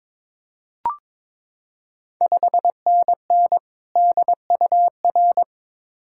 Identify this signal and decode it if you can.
Morse code, heard as a keyed tone, about 22 wpm, 700 Hz. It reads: 5NN DUR
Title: Ontario QSO Party Exchange 22wpm